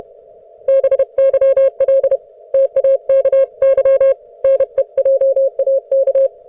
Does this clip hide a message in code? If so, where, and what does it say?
Yes: Morse code from 0.7 to 6.3 s, heard as a keyed tone, about 31 wpm, 540 Hz. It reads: BYL TAKY NEJAK